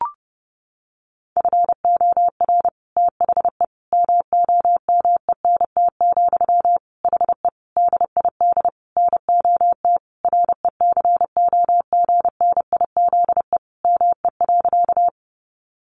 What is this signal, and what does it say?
Morse code, heard as a keyed tone, about 30 wpm, 700 Hz. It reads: FOR THE MOMENT, HE DID NOT RECOGNIZE ME.
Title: Sentences from Top 1400 Words 30wpm